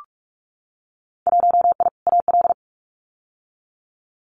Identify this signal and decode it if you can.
Morse code, heard as a keyed tone, about 44 wpm, 700 Hz. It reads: 1I AL
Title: Winter Field Day Exchanges 45wpm